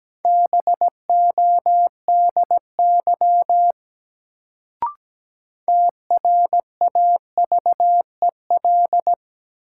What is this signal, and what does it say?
Morse code, heard as a keyed tone, about 17 wpm, 700 Hz. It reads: BODY TRAVEL
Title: Sentences from Top 400 Words - New Word Drill - 17wpm